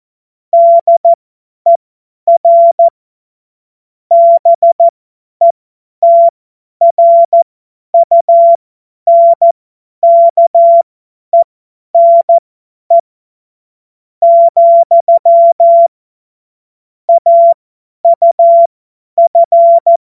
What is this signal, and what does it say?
Morse code, heard as a keyed tone, about 14 wpm, spaced as for 10 wpm, 680 Hz. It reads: DER BETRUNKENE, AUF
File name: Morsekurs / Witze / Witz 01. Juni 2021, 50 F70